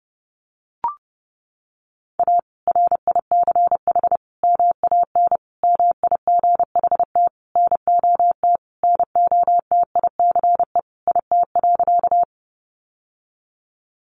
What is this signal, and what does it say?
Morse code, heard as a keyed tone, about 30 wpm, 700 Hz. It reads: A RICH MAN MIGHT NOT NOTICE IT.
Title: Sentences from Top 1100 Words 30wpm